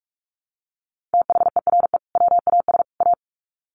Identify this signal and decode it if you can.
Morse code, heard as a keyed tone, about 44 wpm, 700 Hz. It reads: THERE WAS A
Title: Sets of 3 Words 45wpm